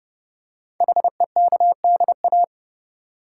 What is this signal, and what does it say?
Morse code, heard as a keyed tone, about 29 wpm, 700 Hz. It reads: HEKDA
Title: Code Groups - 5 Letters - 30wpm